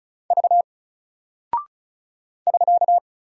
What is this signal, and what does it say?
Morse code, heard as a keyed tone, about 36 wpm, 700 Hz. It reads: V <SK>
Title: Single Character (Letter:Number:Punctuation:Prosign) Alt 35wpm